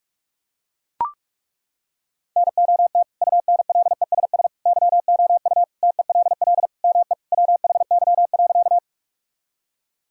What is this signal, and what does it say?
Morse code, heard as a keyed tone, about 41 wpm, 700 Hz. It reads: NOT UNLESS YOU TELL ME WHY.
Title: Sentences from Top 1000 Words - Review Part 6 - 45wpm